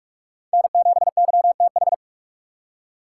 Morse code, heard as a keyed tone, about 44 wpm, 700 Hz. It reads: N7YTH